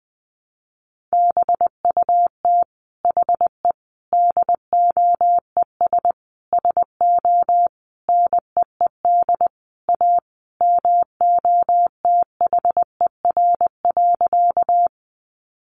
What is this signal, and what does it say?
Morse code, heard as a keyed tone, about 20 wpm, 700 Hz. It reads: BUT HE DOES SO NEED A MOTHER.
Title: Sentences from Top 200 Words 20wpm